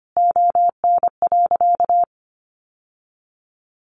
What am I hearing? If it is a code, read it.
Morse code, heard as a keyed tone, about 25 wpm, 700 Hz. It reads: ON.
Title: Sentences from Top 1100 Words 25wpm